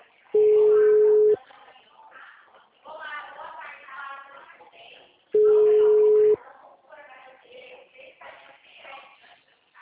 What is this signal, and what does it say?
Ringback tone (Europe, 425 Hz), with speech.